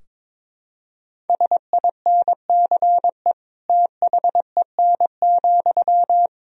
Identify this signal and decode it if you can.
Morse code, heard as a keyed tone, about 22 wpm, 700 Hz. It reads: SINCE THEN,
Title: Sentences from Top 1000 Words - Review Part 3 - 22wpm